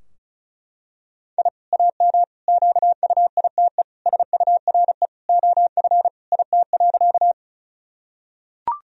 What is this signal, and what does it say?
Morse code, heard as a keyed tone, about 36 wpm, 700 Hz. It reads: I AM QUITE SURE OF IT.
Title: Sentences from Top 1000 Words - Review Part 2 - 35wpm